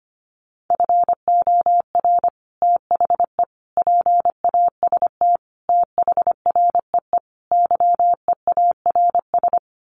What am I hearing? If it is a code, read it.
Morse code, heard as a keyed tone, about 25 wpm, 700 Hz. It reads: FOR THE PAST THREE YEARS
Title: Sets of 5 Words Encore2 25wpm